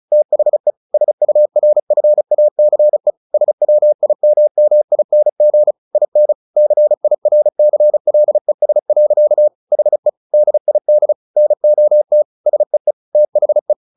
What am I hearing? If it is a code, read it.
Morse code, heard as a keyed tone, about 34 wpm, 600 Hz. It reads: THE SURFACE SWIMMING IN CIRCLES. HE DID NOT SEE THE
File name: the old man and the sea 2 35wpm